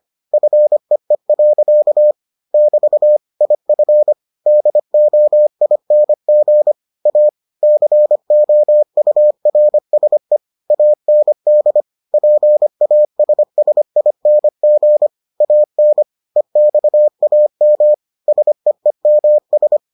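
Morse code, heard as a keyed tone, about 25 wpm, 600 Hz. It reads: FEE. <BT> IF DOING A COURSE AND PASSING AN EXAM SEEMS
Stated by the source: Morse edition of the Foundations of Amateur Radio episodes.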